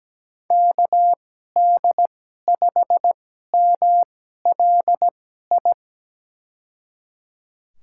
Morse code, heard as a keyed tone, about 17 wpm, spaced as for 12 wpm, 700 Hz. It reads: KD5MLI
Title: USA Call Signs - 6 Characters - 2x Character Spacing - 17wpm